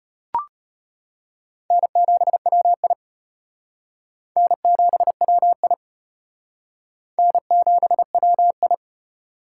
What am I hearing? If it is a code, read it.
Morse code, heard as a keyed tone, about 31 wpm, 700 Hz. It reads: N7WI N7WI N7WI
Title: Speed Racing - USA 1x2 Call Signs 25wpm